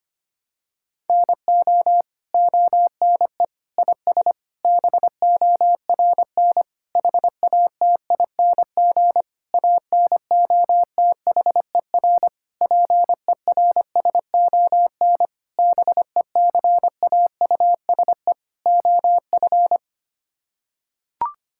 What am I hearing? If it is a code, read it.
Morse code, heard as a keyed tone, about 25 wpm, 700 Hz. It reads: NO ONE IS BORN HATING ANOTHER PERSON BECAUSE OF
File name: Sets of 9 Words - 25wpm